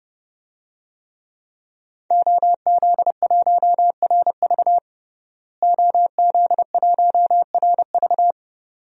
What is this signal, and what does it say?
Morse code, heard as a keyed tone, about 32 wpm, 700 Hz. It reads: OZ1RV OZ1RV
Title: DXCC-100 Call Signs - 5 Characters - Repeated 2x - 30wpm